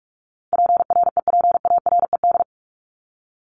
Morse code, heard as a keyed tone, about 44 wpm, 700 Hz. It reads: PREPARED
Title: ED Suffix - Top 100 Words with Base Words 45wpm